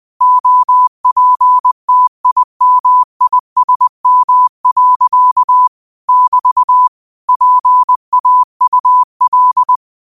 Morse code, heard as a keyed tone, about 20 wpm, 1000 Hz. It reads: OPTIMISM. <BT> PAUL